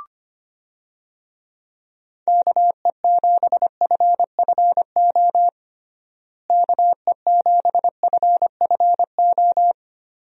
Morse code, heard as a keyed tone, about 25 wpm, 700 Hz. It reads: KE7FFO KE7FFO
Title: USA Call Signs - 6 Characters - Repeated 3x - 25wpm